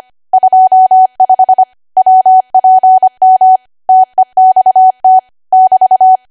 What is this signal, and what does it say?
Morse code, heard as a keyed tone, about 25 wpm, 750 Hz. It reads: 25 WPM TEXT <BT>
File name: cwa / W1AW-Level3 / 25WPM